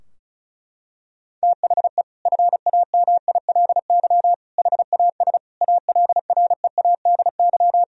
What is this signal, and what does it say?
Morse code, heard as a keyed tone, about 34 wpm, 700 Hz. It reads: THE FAMILY HAS ALREADY
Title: Sentences from Top 1000 Words - Review Part 2 - 35wpm